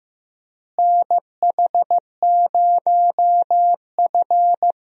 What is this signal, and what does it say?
Morse code, heard as a keyed tone, about 15 wpm, 700 Hz. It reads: NH0F
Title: USA Call Signs - 4 Characters - Repeated 2x - 15wpm